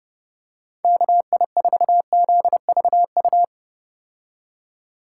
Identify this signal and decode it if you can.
Morse code, heard as a keyed tone, about 30 wpm, 700 Hz. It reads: KI4ZVU
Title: K1USN SST QSO Practice - 30wpm